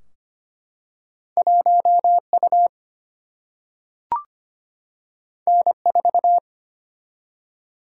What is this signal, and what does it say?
Morse code, heard as a keyed tone, about 25 wpm, 700 Hz. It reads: 1U N4
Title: Single-Letter Single-Number 25wpm